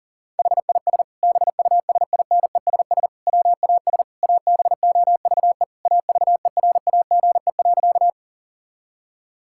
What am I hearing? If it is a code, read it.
Morse code, heard as a keyed tone, about 40 wpm, 700 Hz. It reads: HIS BUSINESS WAS ABOVE AVERAGE.
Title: Sentences from Top 1000 Words - Review Part 6 - 40wpm